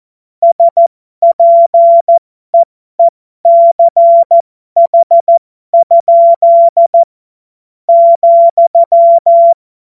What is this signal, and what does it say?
Morse code, heard as a keyed tone, about 14 wpm, spaced as for 12 wpm, 680 Hz. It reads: SPEECH? ,
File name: Morsekurs / Witze